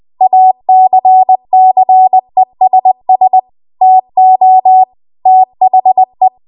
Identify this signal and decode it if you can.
Morse code, heard as a keyed tone, about 20 wpm, 750 Hz. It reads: ACCESS TO THE